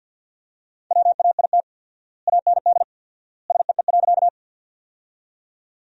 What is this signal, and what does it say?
Morse code, heard as a keyed tone, about 48 wpm, 700 Hz. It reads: WAIT AND SEE.
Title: 4x Word-Spacing - Sentences from Top 500 Words - 50wpm